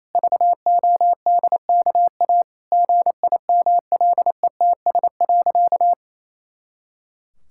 Morse code, heard as a keyed tone, about 28 wpm, 700 Hz. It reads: VODKA GIMLETS.
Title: Pangram Challenge v2 - 28wpm